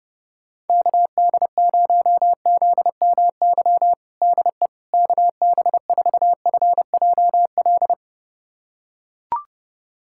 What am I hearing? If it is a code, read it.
Morse code, heard as a keyed tone, about 30 wpm, 700 Hz. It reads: KD0ZMY DE KB4FJL
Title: USA Callsign DE Callsign - 6 Characters - 30wpm